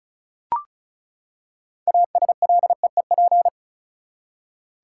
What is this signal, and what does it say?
Morse code, heard as a keyed tone, about 35 wpm, 700 Hz. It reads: ASLEEP
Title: Six Letter Words - 35wpm